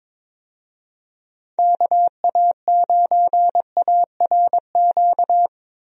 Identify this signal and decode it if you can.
Morse code, heard as a keyed tone, about 22 wpm, 700 Hz. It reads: KA9ARQ
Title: USA Call Signs - 6 Characters - Repeated 2x - 22wpm